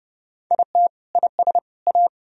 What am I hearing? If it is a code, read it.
Morse code, heard as a keyed tone, about 32 wpm, 700 Hz. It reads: IT IS A
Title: Sentences from Top 1000 Words - Review Part 4 - 30wpm